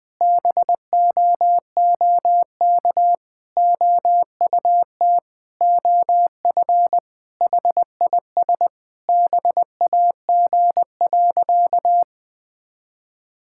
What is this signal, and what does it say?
Morse code, heard as a keyed tone, about 20 wpm, 700 Hz. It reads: BOOK OUT OF HIS BAG.
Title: Sentences with 3-Word Phrases - 20wpm